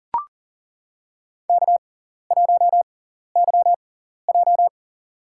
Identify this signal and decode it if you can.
Morse code, heard as a keyed tone, about 40 wpm, spaced as for 14 wpm, 700 Hz. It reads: K1YJ
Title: USA Call Signs - 4 Characters - 6x Character Spacing - 40wpm